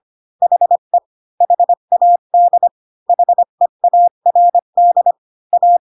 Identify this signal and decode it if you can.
Morse code, heard as a keyed tone, about 25 wpm, 700 Hz. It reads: HE HAD HEARD A
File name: Pinocchio / Pinocchio ch1 20 25 ik1hge